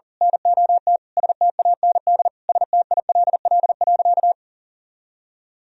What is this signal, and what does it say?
Morse code, heard as a keyed tone, about 40 wpm, 700 Hz. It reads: NOT STAND STILL.
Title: Sentences from Top 1000 Words - Review Part 1 - 40wpm